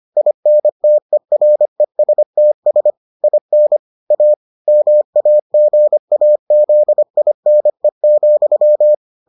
Morse code, heard as a keyed tone, about 25 wpm, 600 Hz. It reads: INTERESTS IN A MAGAZINE,